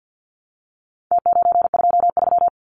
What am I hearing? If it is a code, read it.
Morse code, heard as a keyed tone, about 50 wpm, 700 Hz. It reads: T923
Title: Numbers - Contest Style - 4-Digit 50wpm